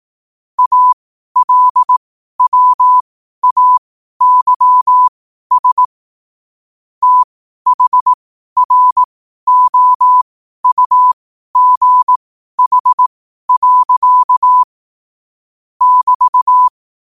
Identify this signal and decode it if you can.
Morse code, heard as a keyed tone, about 18 wpm, spaced as for 12 wpm, 1000 Hz. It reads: ALWAYS THROUGH. <BT>
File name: qotdfiles / 12wpm